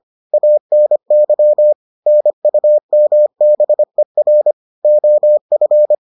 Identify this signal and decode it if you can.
Morse code, heard as a keyed tone, about 25 wpm, 600 Hz. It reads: ANY NUMBER OF